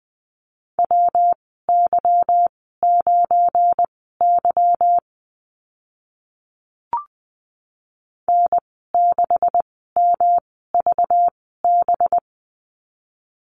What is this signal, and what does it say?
Morse code, heard as a keyed tone, about 20 wpm, spaced as for 14 wpm, 700 Hz. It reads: WY9Y N6MVB
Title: 193 Lesson 31g - Letter v - Call Signs 20wpm